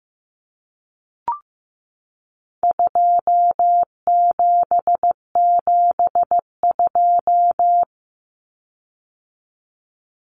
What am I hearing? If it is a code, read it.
Morse code, heard as a keyed tone, about 15 wpm, 700 Hz. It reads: 2772
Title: Common Cut Numbers - 4-Digit 15wpm